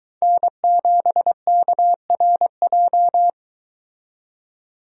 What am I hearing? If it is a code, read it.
Morse code, heard as a keyed tone, about 22 wpm, 700 Hz. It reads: N7KRJ